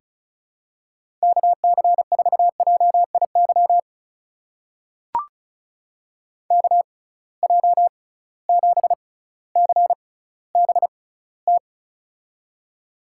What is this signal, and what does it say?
Morse code, heard as a keyed tone, about 36 wpm, spaced as for 18 wpm, 700 Hz. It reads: KC4JIY KJ7CBT
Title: USA Call Signs - 6 Characters - 6x Character Spacing - 35wpm